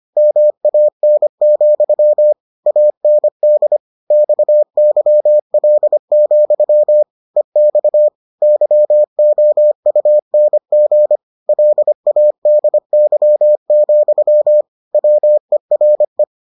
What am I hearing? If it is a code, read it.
Morse code, heard as a keyed tone, about 25 wpm, 600 Hz. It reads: MAN, AND XYL, EX YOUNG LADY, WERE